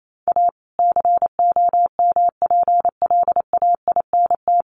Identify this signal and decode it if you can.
Morse code, heard as a keyed tone, about 28 wpm, 700 Hz. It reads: A COMPLAINT
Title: Sentences from Top 1400 Words 28wpm